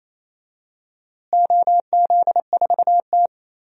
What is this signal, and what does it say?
Morse code, heard as a keyed tone, about 28 wpm, 700 Hz. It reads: OZ4T
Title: DXCC-100 Call Signs - 4 Characters - Repeated 2x - 28wpm